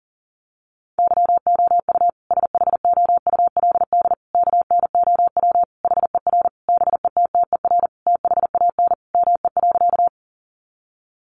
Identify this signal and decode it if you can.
Morse code, heard as a keyed tone, about 40 wpm, 700 Hz. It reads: YOU SHOULD KNOW HER BETTER THAN ME.
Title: Sentences from Top 400 Words 40wpm